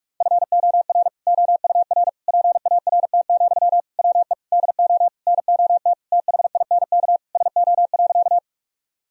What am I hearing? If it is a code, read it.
Morse code, heard as a keyed tone, about 42 wpm, 700 Hz. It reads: FOR OUR PART, WE DO NOT THINK SO.
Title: Sentences from Top 1000 Words - Review Part 1 - 45wpm